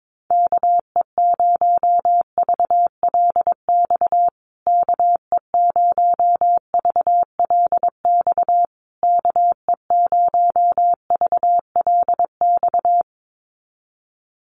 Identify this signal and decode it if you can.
Morse code, heard as a keyed tone, about 22 wpm, 700 Hz. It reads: KE0VLX KE0VLX KE0VLX
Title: USA Call Signs Repeated 3x 22wpm